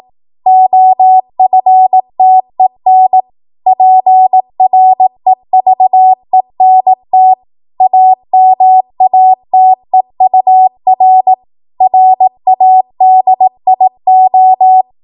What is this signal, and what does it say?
Morse code, heard as a keyed tone, about 18 wpm, 750 Hz. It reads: OFTEN PREVENT AMATEUR RADIO
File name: Morse / 18WPM